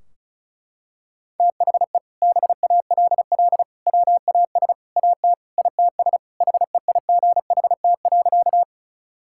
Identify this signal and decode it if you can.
Morse code, heard as a keyed tone, about 36 wpm, 700 Hz. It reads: THE BALL WAS AT ITS HEIGHT.
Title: Sentences from Top 1600 Words 35wpm